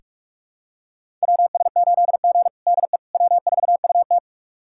Morse code, heard as a keyed tone, about 41 wpm, 700 Hz. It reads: WS8G DE W4UT
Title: USA Callsign DE Callsign - 4 Characters - 45wpm